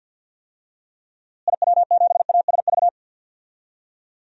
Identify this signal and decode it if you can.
Morse code, heard as a keyed tone, about 47 wpm, 700 Hz. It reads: IW7ASV
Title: DXCC-100 Call Signs - 6 Characters - Repeated 2x - 50wpm